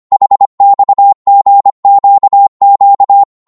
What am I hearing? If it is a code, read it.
Morse code, heard as a keyed tone, about 25 wpm, 800 Hz. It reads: HXGQQ
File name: Morse / ltr-25wpm